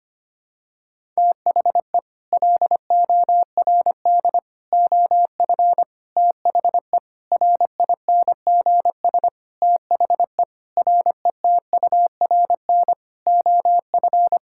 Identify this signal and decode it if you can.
Morse code, heard as a keyed tone, about 25 wpm, 700 Hz. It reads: THE LORD OF THE RINGS THE RETURN OF
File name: Sets of 9 Words - 25wpm